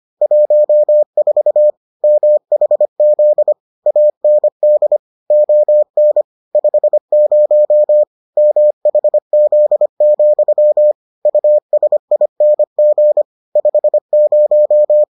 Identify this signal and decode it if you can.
Morse code, heard as a keyed tone, about 25 wpm, 600 Hz. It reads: 14 MHZ AND ON 50 MHZ, USING 50